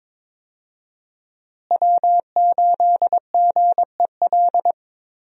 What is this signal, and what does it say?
Morse code, heard as a keyed tone, about 23 wpm, 700 Hz. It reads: W8GEL
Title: USA Call Signs - 5 Characters - Repeated 2x - 22wpm